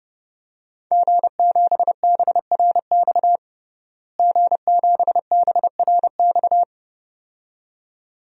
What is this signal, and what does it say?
Morse code, heard as a keyed tone, about 32 wpm, 700 Hz. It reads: G7BRX G7BRX
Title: DXCC-100 Call Signs - 5 Characters - Repeated 2x - 30wpm